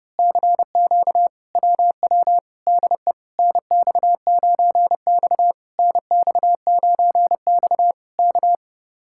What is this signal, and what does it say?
Morse code, heard as a keyed tone, about 30 wpm, 700 Hz. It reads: CQ WW DE NX9X NX9X K
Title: Speed-Racing Common QSO Phrases - 30wpm